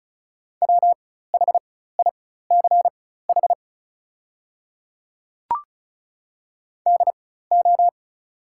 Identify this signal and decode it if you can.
Morse code, heard as a keyed tone, about 36 wpm, spaced as for 16 wpm, 700 Hz. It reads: WHICH DO
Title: 4x Character Spacing - Top 100 Words - 35wpm